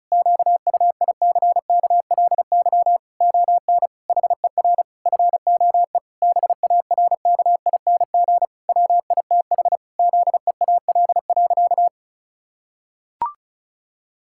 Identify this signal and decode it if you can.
Morse code, heard as a keyed tone, about 34 wpm, 700 Hz. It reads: QUICKLY ON HER FOE BARKING WITH ZEAL.
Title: Pangram Challenge v2 - 35wpm